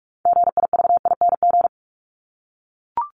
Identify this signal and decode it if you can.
Morse code, heard as a keyed tone, about 44 wpm, 700 Hz. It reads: GIVING
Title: ING Suffix - Top 100 Words with Base Words 45wpm